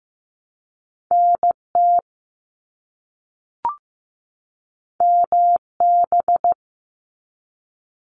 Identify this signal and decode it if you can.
Morse code, heard as a keyed tone, about 15 wpm, 700 Hz. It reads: NT MB
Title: Canadian Provinces and Territories 15wpm